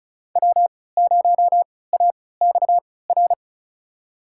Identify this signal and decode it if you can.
Morse code, heard as a keyed tone, about 35 wpm, spaced as for 20 wpm, 690 Hz. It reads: W0AXR
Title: USA Call Signs - 3x Character Spacing - 35wpm